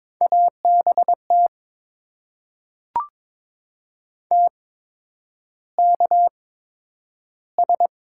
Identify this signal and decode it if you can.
Morse code, heard as a keyed tone, about 23 wpm, spaced as for 10 wpm, 690 Hz. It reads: ABT TKS
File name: Top 90 QSO Elements - 8x Character Spacing - 22wpm